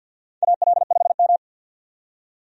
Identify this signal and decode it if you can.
Morse code, heard as a keyed tone, about 53 wpm, 700 Hz. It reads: AL5M
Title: USA Call Signs - 4 Characters - 50wpm